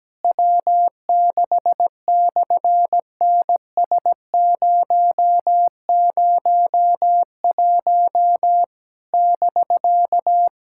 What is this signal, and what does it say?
Morse code, heard as a keyed tone, about 17 wpm, 700 Hz. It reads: W6/NS001 <BK>
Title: SOTA QSO Practice 17wpm